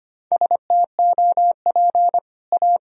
Morse code, heard as a keyed tone, about 25 wpm, 700 Hz. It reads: STOP A
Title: Sentences from Top 1000 Words - Review Part 2 - 25wpm